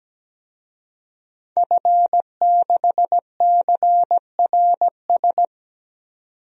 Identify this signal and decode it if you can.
Morse code, heard as a keyed tone, about 17 wpm, 700 Hz. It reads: F6CRS
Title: DXCC-100 Call Signs - 5 Characters - 17wpm